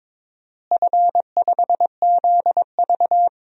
Morse code, heard as a keyed tone, about 23 wpm, 700 Hz. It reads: F5ZV
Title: DXCC-100 Call Signs - 4 Characters - Repeated 2x - 22wpm